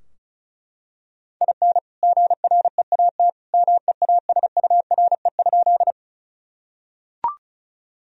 Morse code, heard as a keyed tone, about 36 wpm, 700 Hz. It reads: IN GREAT MEASURE?
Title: Sentences from Top 1000 Words - Review Part 2 - 35wpm